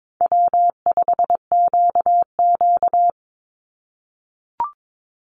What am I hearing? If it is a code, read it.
Morse code, heard as a keyed tone, about 22 wpm, 700 Hz. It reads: W5QQ
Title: Call Signs - 22wpm